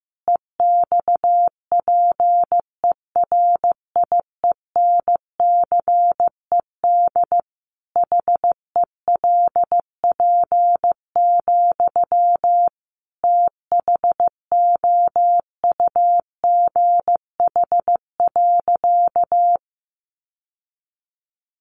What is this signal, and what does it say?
Morse code, heard as a keyed tone, about 15 wpm, 700 Hz. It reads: EXPERIENCED HELP, THOUGH.
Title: Sentences from Top 1100 Words 15wpm